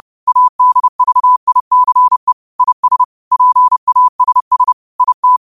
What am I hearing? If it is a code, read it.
Morse code, heard as a keyed tone, about 30 wpm, 1000 Hz. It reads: ADVICE IS PASS IT